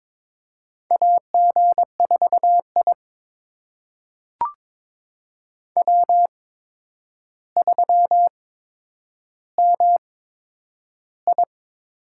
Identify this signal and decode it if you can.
Morse code, heard as a keyed tone, about 22 wpm, spaced as for 9 wpm, 700 Hz. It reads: AG4I W3MI
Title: USA Call Signs - 4 Characters - 8x Character Spacing - 22wpm